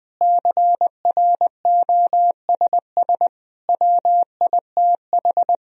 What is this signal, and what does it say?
Morse code, heard as a keyed tone, about 21 wpm, 700 Hz. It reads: CROSS WITH
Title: Sentences from Top 1000 Words - Review Part 1 - 20wpm